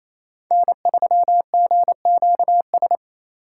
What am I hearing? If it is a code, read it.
Morse code, heard as a keyed tone, about 28 wpm, 700 Hz. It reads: N3GQS
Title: USA Call Signs - 5 Characters - 2x Character Spacing - 28wpm